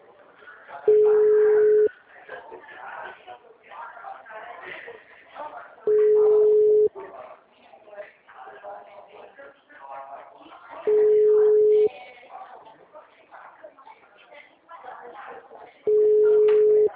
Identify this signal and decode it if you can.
Ringback tone (Europe, 425 Hz), with speech.